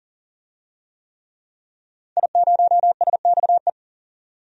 Morse code, heard as a keyed tone, about 37 wpm, 700 Hz. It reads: I0SXE
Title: DXCC-100 Call Signs - 5 Characters - 40wpm